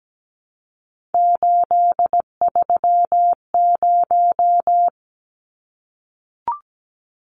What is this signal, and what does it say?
Morse code, heard as a keyed tone, about 17 wpm, 700 Hz. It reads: 830
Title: Numbers - 3-Digit 17wpm